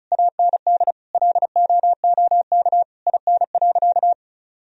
Morse code, heard as a keyed tone, about 36 wpm, 700 Hz. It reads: AND LOOK IN.
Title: Sentences from Top 1000 Words - Review Part 2 - 35wpm